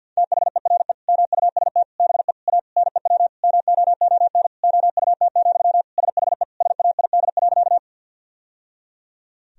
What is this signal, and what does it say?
Morse code, heard as a keyed tone, about 50 wpm, 700 Hz. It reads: THERE MUST BE A NEW MOON OUT, SHE SAID.
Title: Sentences from Top 1000 Words - Review Part 3 - 50wpm